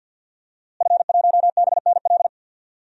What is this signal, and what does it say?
Morse code, heard as a keyed tone, about 50 wpm, 700 Hz. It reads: F1BNL
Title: DXCC-100 Call Signs - 5 Characters - Repeated 2x - 50wpm